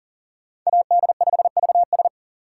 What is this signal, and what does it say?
Morse code, heard as a keyed tone, about 40 wpm, 700 Hz. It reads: AD5VS
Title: USA Call Signs - 6x Character Spacing - 40wpm